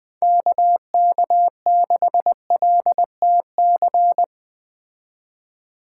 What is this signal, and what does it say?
Morse code, heard as a keyed tone, about 20 wpm, 700 Hz. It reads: KK6LTC